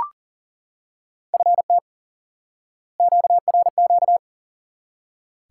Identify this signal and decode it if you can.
Morse code, heard as a keyed tone, about 40 wpm, 700 Hz. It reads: FT QRQ
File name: IQE Course 3 - Lesson 89 - YAGI - Two Elements - 40wpm